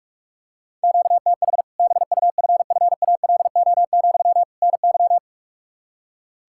Morse code, heard as a keyed tone, about 44 wpm, 700 Hz. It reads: QTH BUFFALO, NY
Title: Practice QSO - Exchange 1 - Beginning - 45wpm